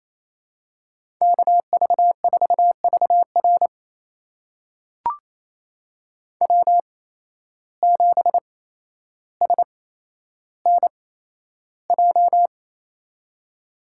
Morse code, heard as a keyed tone, about 28 wpm, spaced as for 12 wpm, 700 Hz. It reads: KV4VR W7SNJ
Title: USA Call Signs - 5 Characters - 8x Character Spacing - 28wpm